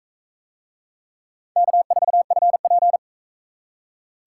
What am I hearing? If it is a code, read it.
Morse code, heard as a keyed tone, about 41 wpm, 700 Hz. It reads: K4FP